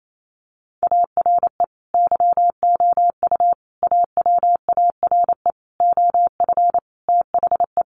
Morse code, heard as a keyed tone, about 28 wpm, 700 Hz. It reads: ARE YOU AWARE OF THE
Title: Sentences from Top 1500 Words 28wpm